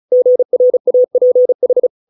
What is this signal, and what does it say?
Morse code, heard as a keyed tone, about 34 wpm, 500 Hz. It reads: GRAPH